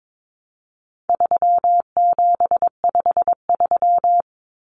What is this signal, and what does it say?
Morse code, heard as a keyed tone, about 22 wpm, 700 Hz. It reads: 3753
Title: Common Cut Numbers - 4-Digit 22wpm